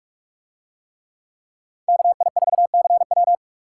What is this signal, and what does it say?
Morse code, heard as a keyed tone, about 48 wpm, 700 Hz. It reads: KI4CW